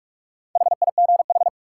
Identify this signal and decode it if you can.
Morse code, heard as a keyed tone, about 46 wpm, 700 Hz. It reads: HIGH
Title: Sentences from Top 200 Words - New Word Drill - 45wpm